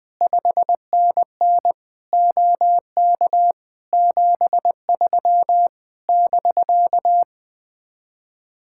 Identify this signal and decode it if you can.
Morse code, heard as a keyed tone, about 20 wpm, 700 Hz. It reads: 5NN OK 73 <BK>
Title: Speed-Racing Common QSO Phrases - 17wpm